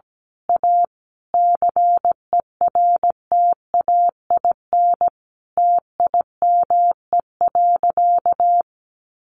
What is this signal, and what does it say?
Morse code, heard as a keyed tone, about 17 wpm, 700 Hz. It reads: A CERTAIN TIME.
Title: Sentences from Top 1000 Words 17wpm